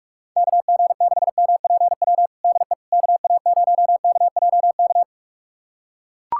Morse code, heard as a keyed tone, about 42 wpm, 700 Hz. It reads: KG6MPW DE KA0KJK
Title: USA Callsign DE Callsign - 6 Characters - 45wpm